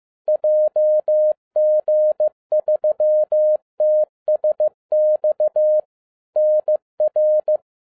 Morse code, heard as a keyed tone, about 15 wpm, 610 Hz. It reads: JG3TSX NR
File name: 2011-07-mp3 / n22 QRP OTAKU 15wpm